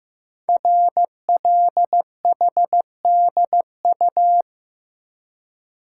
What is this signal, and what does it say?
Morse code, heard as a keyed tone, about 15 wpm, 700 Hz. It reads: RLHDU
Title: Code Groups - 5 Letters - 15wpm